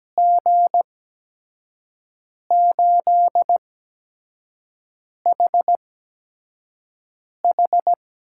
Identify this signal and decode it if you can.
Morse code, heard as a keyed tone, about 17 wpm, spaced as for 5 wpm, 700 Hz. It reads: G8HH
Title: USA Call Signs - 5 Characters - 8x Character Spacing - 17wpm